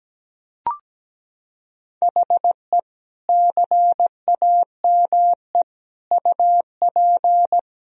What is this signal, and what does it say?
Morse code, heard as a keyed tone, about 17 wpm, 700 Hz. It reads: HE CAME UP
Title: Sentences from Top 1000 Words - Review Part 4 - 17wpm